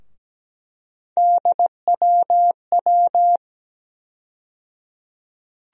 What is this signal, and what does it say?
Morse code, heard as a keyed tone, about 17 wpm, 700 Hz. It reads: DWW
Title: USA Call Sign Postfixes - 3 Letters 17wpm